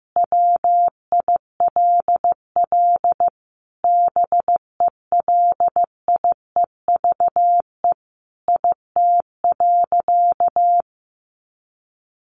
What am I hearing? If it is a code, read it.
Morse code, heard as a keyed tone, about 15 wpm, 700 Hz. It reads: WILL BELIEVE IT.
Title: Sentences from Top 1000 Words 15wpm